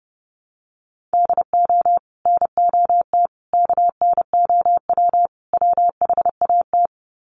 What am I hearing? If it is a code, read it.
Morse code, heard as a keyed tone, about 30 wpm, 700 Hz. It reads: DO NOT KNOW WHAT
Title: Sets of 4 Words 30wpm